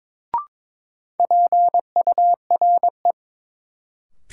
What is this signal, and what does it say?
Morse code, heard as a keyed tone, about 22 wpm, 700 Hz. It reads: PURE
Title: Sentences from Top 1500 Words - New Word Drill - 22wpm